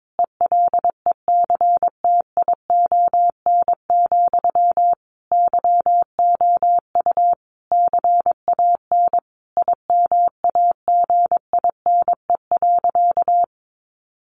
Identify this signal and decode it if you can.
Morse code, heard as a keyed tone, about 22 wpm, 700 Hz. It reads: ELECTION, YOU CAN IMAGINE.
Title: Sentences from Top 1500 Words 22wpm